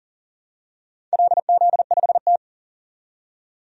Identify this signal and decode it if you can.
Morse code, heard as a keyed tone, about 40 wpm, 700 Hz. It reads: LZ5T
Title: DXCC-100 Call Signs - 4 Characters - 40wpm